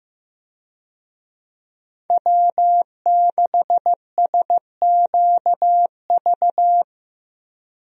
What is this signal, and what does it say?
Morse code, heard as a keyed tone, about 16 wpm, 700 Hz. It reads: W6SQV
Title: USA Call Signs - 5 Characters - Repeated 3x - 15wpm